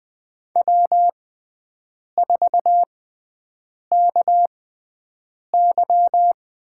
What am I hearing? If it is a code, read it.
Morse code, heard as a keyed tone, about 21 wpm, spaced as for 7 wpm, 700 Hz. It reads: W4KY
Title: USA Call Signs - 6x Character Spacing - 20wpm